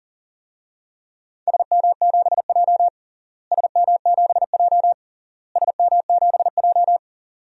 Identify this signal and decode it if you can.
Morse code, heard as a keyed tone, about 40 wpm, 700 Hz. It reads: SM7J SM7J SM7J
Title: DXCC-100 Call Signs - 4 Characters - Repeated 3x - 40wpm